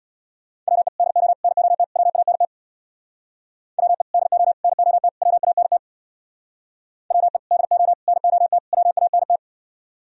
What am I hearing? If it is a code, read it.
Morse code, heard as a keyed tone, about 61 wpm, 700 Hz. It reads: WE DO NOT WANT WE DO NOT WANT WE DO NOT WANT
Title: Speed-Racing - Sets of 4 Words 50wpm